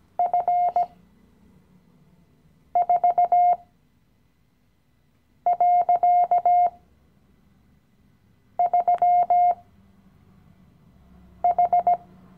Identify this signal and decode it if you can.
Morse code, heard as a keyed tone, about 17 wpm, spaced as for 4 wpm, 710 Hz. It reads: F4.3H
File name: CW-Online-Kurs / Anfänger / Lektion 015